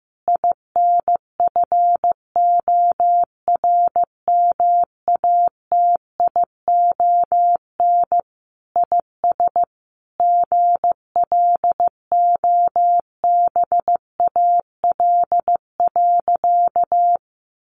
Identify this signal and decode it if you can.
Morse code, heard as a keyed tone, about 15 wpm, 700 Hz. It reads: INFORMATION IS GLOBAL.
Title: Sentences from Top 1200 Words 15wpm